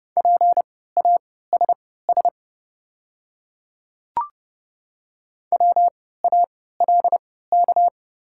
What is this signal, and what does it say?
Morse code, heard as a keyed tone, about 30 wpm, spaced as for 17 wpm, 690 Hz. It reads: PASS WALK
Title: 3x Character Spacing - Top 500 Words - 30wpm